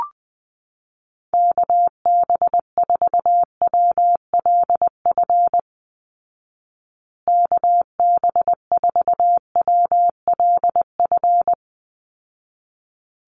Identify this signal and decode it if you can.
Morse code, heard as a keyed tone, about 20 wpm, 700 Hz. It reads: KB4WLF KB4WLF
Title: Call Signs Repeated 3x - 20wpm